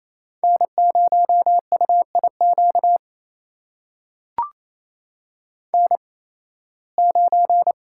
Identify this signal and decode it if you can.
Morse code, heard as a keyed tone, about 28 wpm, spaced as for 18 wpm, 700 Hz. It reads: N0UIQ N9
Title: USA Call Signs - 5 Characters - 8x Character Spacing - 28wpm